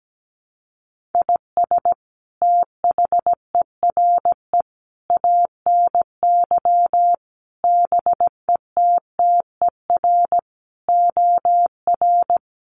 Morse code, heard as a keyed tone, about 17 wpm, 700 Hz. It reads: IS THERE ANY BETTER OR
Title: Sentences from Top 1400 Words 17wpm